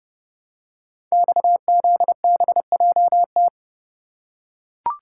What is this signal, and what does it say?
Morse code, heard as a keyed tone, about 30 wpm, 700 Hz. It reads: XZBJT
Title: Code Groups - 5 Letter:Numbers - 30wpm